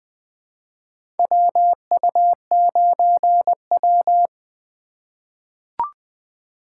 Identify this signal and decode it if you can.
Morse code, heard as a keyed tone, about 20 wpm, 700 Hz. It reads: WU9W
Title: USA Call Signs - 4 Characters - 4x Character Spacing - 20wpm